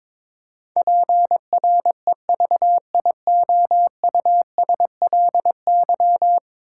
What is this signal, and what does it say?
Morse code, heard as a keyed tone, about 22 wpm, 700 Hz. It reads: PREVIOUSLY
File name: practice / 22wpm / 10 Letter Words - First 3 Letters Given - 22wpm